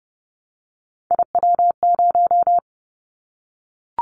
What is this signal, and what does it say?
Morse code, heard as a keyed tone, about 30 wpm, 700 Hz. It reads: IW0
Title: Callsign Prefixes for Top 50 DXCC - 30wpm